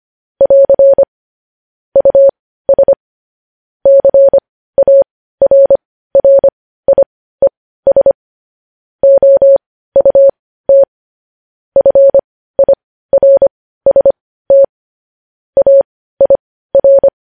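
Morse code, heard as a keyed tone, about 25 wpm, spaced as for 15 wpm, 550 Hz. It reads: <AR> US CARRIES OUT FIRST AIR
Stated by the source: Morse news headlines practice sent at 15 WPM.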